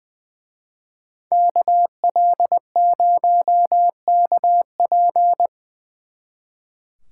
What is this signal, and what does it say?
Morse code, heard as a keyed tone, about 24 wpm, 700 Hz. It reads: KL0KP
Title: Speed Racing - USA 2x2 Call Signs 20wpm